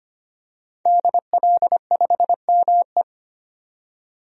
Morse code, heard as a keyed tone, about 25 wpm, 700 Hz. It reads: DL5ME
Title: DXCC-100 Call Signs - 5 Characters - 25wpm